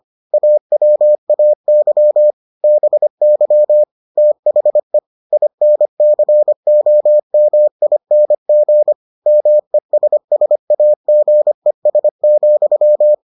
Morse code, heard as a keyed tone, about 25 wpm, 600 Hz. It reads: AWAY BY THE INCOMING MESSAGES,